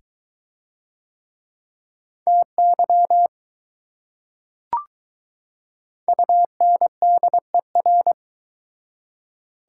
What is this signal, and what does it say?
Morse code, heard as a keyed tone, about 22 wpm, 700 Hz. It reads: TY UNDER